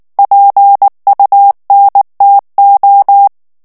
Morse code, heard as a keyed tone, about 19 wpm, 800 Hz. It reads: PUNTO
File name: morse mp3 / ITALIEN texte clair 2 vitesse 18 mots minute